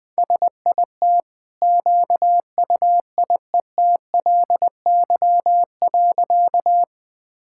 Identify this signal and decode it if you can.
Morse code, heard as a keyed tone, about 20 wpm, 700 Hz. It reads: SIT QUIETLY.
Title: Easy Sentences with Hints - 20wpm